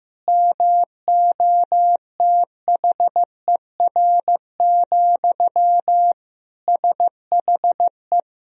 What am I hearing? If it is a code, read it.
Morse code, heard as a keyed tone, about 15 wpm, 700 Hz. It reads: MOTHER, SHE
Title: Sentences from Top 1000 Words - Review Part 4 - 15wpm